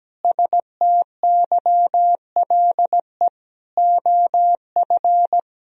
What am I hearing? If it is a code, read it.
Morse code, heard as a keyed tone, about 17 wpm, 700 Hz. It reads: STYLE OF
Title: Sentences from Top 1000 Words - Review Part 4 - 17wpm